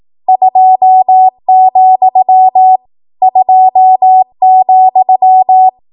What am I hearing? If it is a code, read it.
Morse code, heard as a keyed tone, about 18 wpm, 750 Hz. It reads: 2, 2,
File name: Morse / 18WPM